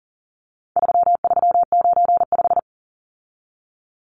Morse code, heard as a keyed tone, about 40 wpm, 700 Hz. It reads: 3395
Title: Numbers - 4-Digit 40wpm